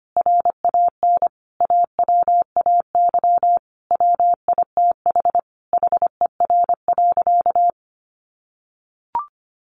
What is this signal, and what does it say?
Morse code, heard as a keyed tone, about 25 wpm, 700 Hz. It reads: RAN AWAY WITH HER.
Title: Sentences from Top 1500 Words 25wpm